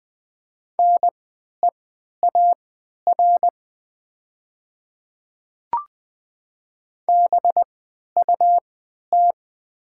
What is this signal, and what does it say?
Morse code, heard as a keyed tone, about 20 wpm, spaced as for 11 wpm, 700 Hz. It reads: NEAR BUT
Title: 3x Character Spacing - Top 200 Words - 20wpm